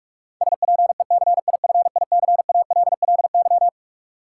Morse code, heard as a keyed tone, about 46 wpm, 700 Hz. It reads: SPECIFICALLY
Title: Sentences from Top 1100 Words - New Word Drill - 45wpm